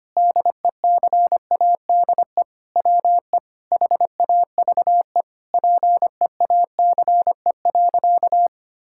Morse code, heard as a keyed tone, about 25 wpm, 700 Hz. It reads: DECADE WE HAVE PEACE.
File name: practice / 25wpm / Sentences with 9-Word Phrases - 25wpm